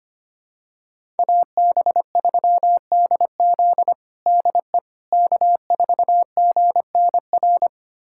Morse code, heard as a keyed tone, about 25 wpm, 700 Hz. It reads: AB3DZ DE K4GNR
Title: USA Callsign DE Callsign - 5 Characters - 25wpm